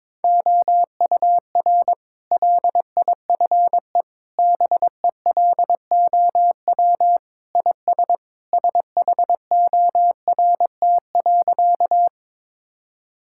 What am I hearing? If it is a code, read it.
Morse code, heard as a keyed tone, about 22 wpm, 700 Hz. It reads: OUR LIFE BELOW IS SHORT.
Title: Sentences from Top 1000 Words - Review Part 5 - 22wpm